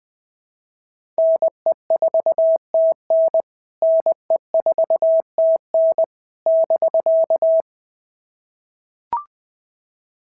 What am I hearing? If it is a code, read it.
Morse code, heard as a keyed tone, about 20 wpm, 650 Hz. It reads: NE4TN NE4TN <BK>
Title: POTA QSO Practice - 20wpm